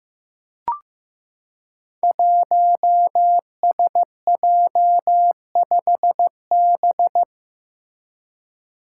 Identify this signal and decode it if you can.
Morse code, heard as a keyed tone, about 15 wpm, 700 Hz. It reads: 1SJ5B
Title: Five Challenging Letter-Number Combinations 15wpm